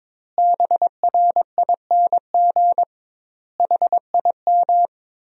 Morse code, heard as a keyed tone, about 22 wpm, 700 Hz. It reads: BRING HIM
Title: 2x Word-Spacing - Sentences from Top 500 Words - 22wpm